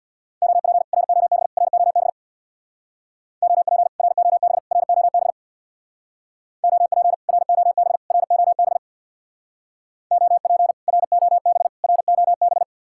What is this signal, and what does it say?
Morse code, heard as a keyed tone, about 67 wpm, 700 Hz. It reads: OP ROB ROB OP ROB ROB OP ROB ROB OP ROB ROB